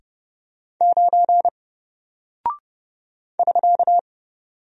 Morse code, heard as a keyed tone, about 29 wpm, 700 Hz. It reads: 9 <SK>
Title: Single Character (Letter:Number:Punctuation:Prosign) Alt 30wpm